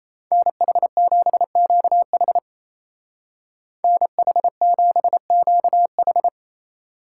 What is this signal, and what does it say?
Morse code, heard as a keyed tone, about 34 wpm, 700 Hz. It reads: NH7QH NH7QH
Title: Speed Racing - USA 2x2 Call Signs 28wpm